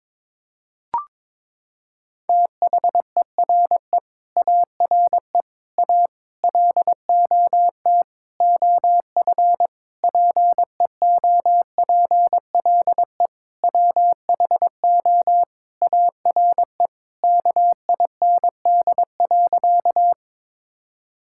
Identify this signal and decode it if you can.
Morse code, heard as a keyed tone, about 22 wpm, 700 Hz. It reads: THERE ARE A LOT OF PEOPLE WHO ARE KIND.
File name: practice / 22wpm / Sentences with 8-Word Phrases - 22wpm